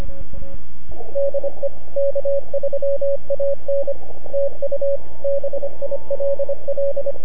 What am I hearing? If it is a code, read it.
Morse code, heard as a keyed tone, about 27 wpm, 530 Hz, with radio noise. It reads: DE K3AN TU BILL
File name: NAQPCW-8-2009 / W1UJ NAQP CW 09 0100Z